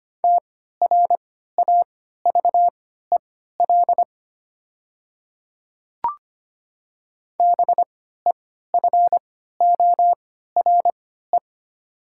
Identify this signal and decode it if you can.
Morse code, heard as a keyed tone, about 25 wpm, spaced as for 14 wpm, 700 Hz. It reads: TRAVEL BEFORE
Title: 3x Character Spacing - Top 500 Words - 25wpm